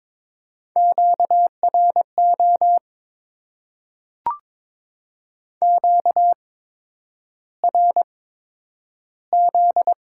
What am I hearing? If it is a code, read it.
Morse code, heard as a keyed tone, about 22 wpm, spaced as for 9 wpm, 700 Hz. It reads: QRO QRZ
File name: Top 90 QSO Elements - 8x Character Spacing - 22wpm